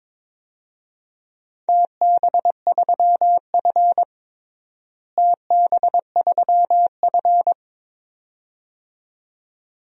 Morse code, heard as a keyed tone, about 22 wpm, 700 Hz. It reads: TB3F TB3F
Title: DXCC-100 Call Signs - 4 Characters - Repeated 2x - 22wpm